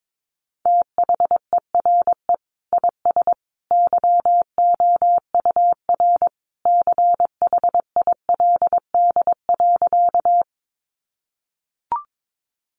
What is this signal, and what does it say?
Morse code, heard as a keyed tone, about 22 wpm, 700 Hz. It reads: THERE IS YOUR CHILD.
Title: Sentences from Top 800 Words 22wpm